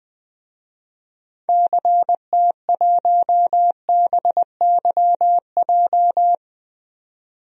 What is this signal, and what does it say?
Morse code, heard as a keyed tone, about 21 wpm, 700 Hz. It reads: CT1BYJ
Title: DXCC-100 Call Signs - 6 Characters - 20wpm